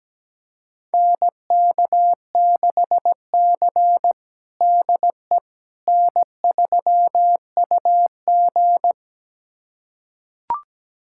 Morse code, heard as a keyed tone, about 17 wpm, 700 Hz. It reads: NK6C DE N3UG
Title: USA Callsign DE Callsign - 4 Characters - 17wpm